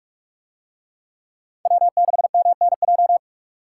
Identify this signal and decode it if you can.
Morse code, heard as a keyed tone, about 44 wpm, 700 Hz. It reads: W6MNJ